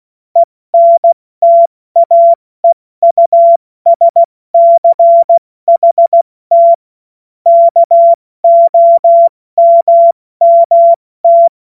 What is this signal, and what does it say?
Morse code, heard as a keyed tone, about 16 wpm, 680 Hz. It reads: ENTAEUSCHT KOMMT
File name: Morsekurs / Witze / Witz 28.November 2023 70 F80